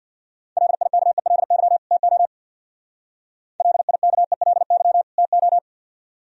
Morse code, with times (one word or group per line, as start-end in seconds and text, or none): 0.6-1.8 LIKELY
1.9-2.3 TO
3.6-5.0 LIKELY
5.2-5.6 TO